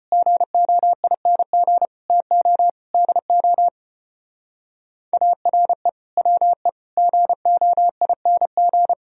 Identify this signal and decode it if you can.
Morse code, heard as a keyed tone, about 29 wpm, 700 Hz. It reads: GOING TO DO ARE WE GOING
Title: Speed-Racing - Sets of 5 Words 25wpm